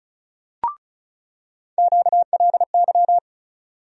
Morse code, heard as a keyed tone, about 34 wpm, 700 Hz. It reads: QLY
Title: USA Call Sign Postfixes - 3 Letters 35wpm